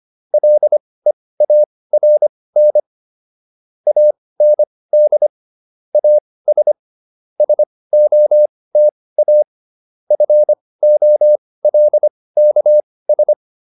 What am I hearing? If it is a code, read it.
Morse code, heard as a keyed tone, about 25 wpm, spaced as for 18 wpm, 600 Hz. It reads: LEARN AND AS SOTA FOLKS